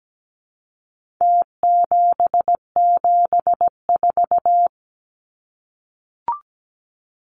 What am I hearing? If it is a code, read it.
Morse code, heard as a keyed tone, about 17 wpm, 700 Hz. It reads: T774
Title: Numbers - Contest Style - 4-Digit 17wpm